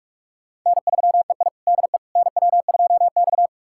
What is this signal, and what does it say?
Morse code, heard as a keyed tone, about 44 wpm, 700 Hz. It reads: N3EI DE NW2X
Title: USA Callsign DE Callsign - 4 Characters - 45wpm